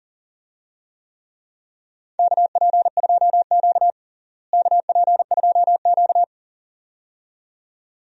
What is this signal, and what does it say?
Morse code, heard as a keyed tone, about 40 wpm, 700 Hz. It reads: KP2Q KP2Q
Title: DXCC-100 Call Signs - 4 Characters - Repeated 2x - 40wpm